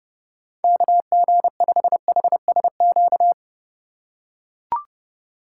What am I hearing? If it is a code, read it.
Morse code, heard as a keyed tone, about 30 wpm, 700 Hz. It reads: KG5HSQ
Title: USA Call Signs - 6 Characters - 4x Character Spacing - 30wpm